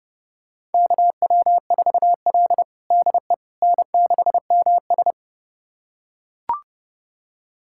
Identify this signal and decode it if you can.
Morse code, heard as a keyed tone, about 30 wpm, 700 Hz. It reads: KW4L DE N6MS
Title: USA Callsign DE Callsign - 4 Characters - 30wpm